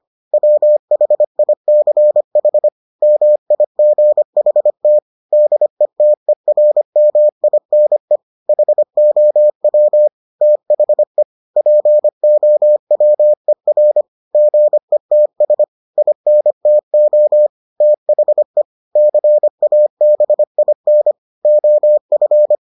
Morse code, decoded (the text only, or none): WHICH MIGHT DETERMINE HOW THE POWER GETS INTO THE CABIN OF